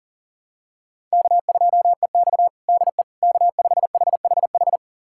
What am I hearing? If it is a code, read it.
Morse code, heard as a keyed tone, about 37 wpm, 700 Hz. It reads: K2EX DE K5HHH
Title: USA Callsign DE Callsign 40wpm